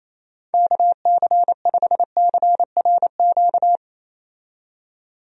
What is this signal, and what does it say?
Morse code, heard as a keyed tone, about 28 wpm, 700 Hz. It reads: KC5CRQ
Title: USA Call Signs - 6 Characters - 28wpm